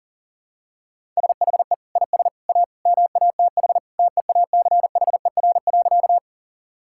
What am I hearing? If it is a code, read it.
Morse code, heard as a keyed tone, about 40 wpm, 700 Hz. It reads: SHE IS A MATH TEACHER.
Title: Sentences from Top 1000 Words - Review Part 6 - 40wpm